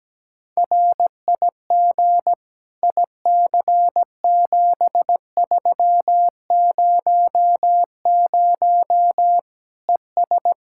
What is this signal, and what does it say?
Morse code, heard as a keyed tone, about 17 wpm, 700 Hz. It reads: RIG IC7300 ES
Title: K1USN SST QSO Practice 17wpm